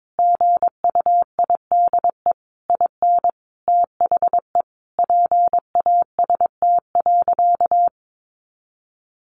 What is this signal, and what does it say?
Morse code, heard as a keyed tone, about 22 wpm, 700 Hz. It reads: GUIDE IN THE PAST.
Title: A Sound Of Thunder 22wpm